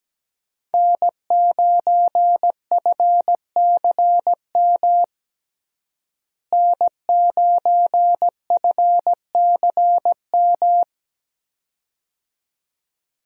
Morse code, heard as a keyed tone, about 17 wpm, 700 Hz. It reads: N9FCM N9FCM
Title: USA Call Signs - 5 Characters - Repeated 2x - 17wpm